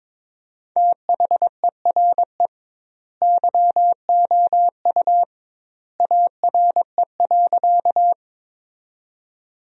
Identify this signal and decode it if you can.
Morse code, heard as a keyed tone, about 22 wpm, 700 Hz. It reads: THERE YOU ARE.
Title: 2x Word-Spacing - Sentences from Top 100 Words - 22wpm